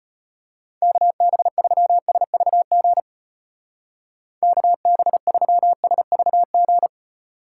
Morse code, heard as a keyed tone, about 32 wpm, 700 Hz. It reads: KB3SVG KB3SVG